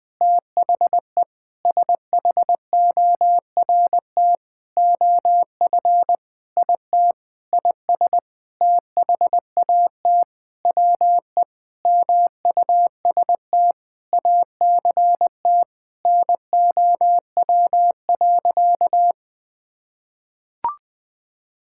Morse code, heard as a keyed tone, about 20 wpm, 700 Hz. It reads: THE SHORT OF IT IS THAT WE MUST ACT NOW.
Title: Sentences with 9-Word Phrases - Spoken Beginning - 20wpm